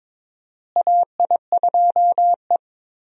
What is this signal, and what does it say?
Morse code, heard as a keyed tone, about 22 wpm, 700 Hz. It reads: AI2E
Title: USA Call Signs - 4 Characters - Repeated 3x - 22wpm